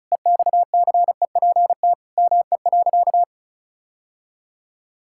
Morse code, heard as a keyed tone, about 34 wpm, 700 Hz. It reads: EXCEPT ME.
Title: Sentences from Top 1600 Words 35wpm